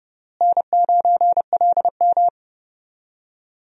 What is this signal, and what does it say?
Morse code, heard as a keyed tone, about 30 wpm, 700 Hz. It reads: N9LM
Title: USA Call Signs - 4 Characters - Repeated 3x - 30wpm